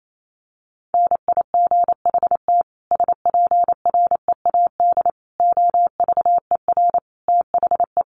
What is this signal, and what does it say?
Morse code, heard as a keyed tone, about 28 wpm, 700 Hz. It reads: NIGHT SPREAD OVER THE
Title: Sentences from Top 1200 Words 28wpm